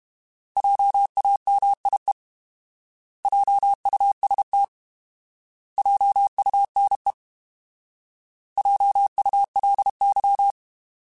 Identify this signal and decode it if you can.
Morse code, heard as a keyed tone, about 32 wpm, 800 Hz. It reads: JAMIE JUST JUNE JULY
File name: Alphabetical / Alphabetical mp3s / Js @30